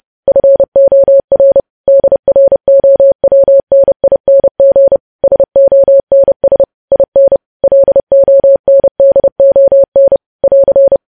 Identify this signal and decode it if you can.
Morse code, heard as a keyed tone, about 30 wpm, 550 Hz. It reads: FOR DROWNING SONS IN LONDON <AR>
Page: news headlines in morse code at 30 wpm